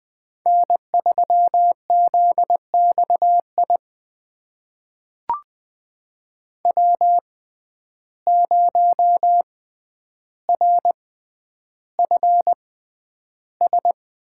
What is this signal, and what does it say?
Morse code, heard as a keyed tone, about 20 wpm, spaced as for 10 wpm, 700 Hz. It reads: N3ZXI W0RFS
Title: USA Call Signs - 5 Characters - 6x Character Spacing - 20wpm